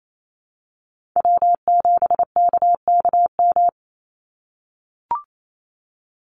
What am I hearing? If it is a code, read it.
Morse code, heard as a keyed tone, about 28 wpm, 700 Hz. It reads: W7KKM
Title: NA SOTA Chaser Callsigns 28wpm